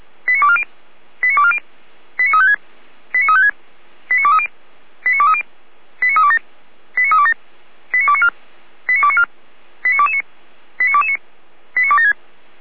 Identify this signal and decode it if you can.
Five-tone selective call (ZVEI), heard as quick runs of five tones, tones of about 70 ms. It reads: 89240 89240 89356 89356 89230 89230 89327 89327 89294 89294 89229 89229 89276